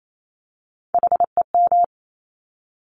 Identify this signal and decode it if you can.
Morse code, heard as a keyed tone, about 28 wpm, 700 Hz. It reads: HEM
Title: Top Three Letter Combinations in Words 28wpm